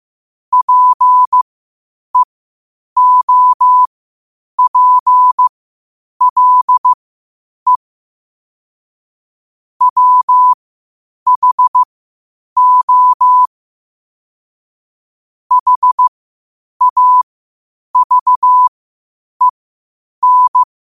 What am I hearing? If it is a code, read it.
Morse code, heard as a keyed tone, about 15 wpm, spaced as for 8 wpm, 1000 Hz. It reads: PEOPLE WHO HAVEN
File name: qotdfiles / 8wpm